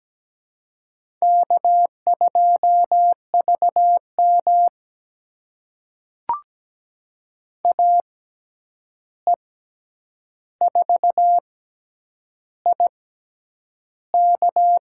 Morse code, heard as a keyed tone, about 17 wpm, spaced as for 8 wpm, 700 Hz. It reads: K2VM AE4IK
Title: USA Call Signs - 6x Character Spacing - 17wpm